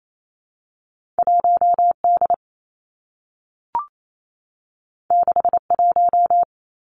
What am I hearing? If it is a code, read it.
Morse code, heard as a keyed tone, about 28 wpm, 700 Hz. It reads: 1 D 6 1
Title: Two Challenging Letter-Number Combinations 28wpm